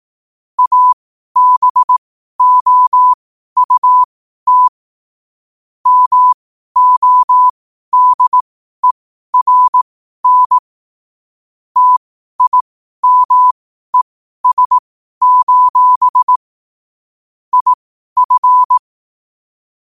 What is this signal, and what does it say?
Morse code, heard as a keyed tone, about 18 wpm, spaced as for 12 wpm, 1000 Hz. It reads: ABOUT MODERN TIMES: IF
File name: 12wpm